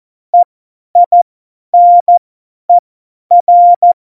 Morse code, heard as a keyed tone, about 14 wpm, spaced as for 10 wpm, 700 Hz. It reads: EINER
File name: Morsekurs / Witze / Witz 28. November 2017, 50 F70